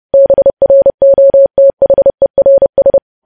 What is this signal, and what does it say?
Morse code, heard as a keyed tone, about 30 wpm, 550 Hz. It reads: BROTHERS
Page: news headlines in morse code at 30 wpm